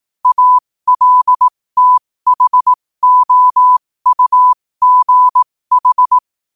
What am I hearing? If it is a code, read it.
Morse code, heard as a keyed tone, about 18 wpm, spaced as for 15 wpm, 1000 Hz. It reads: ALTHOUGH
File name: qotdfiles / 15wpm